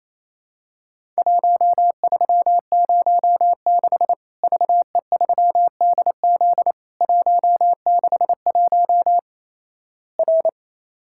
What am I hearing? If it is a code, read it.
Morse code, heard as a keyed tone, about 28 wpm, 700 Hz. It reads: 1306 VE3DZ 161 R
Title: Worked All Europe DX Contest - QTC Exchange - 28wpm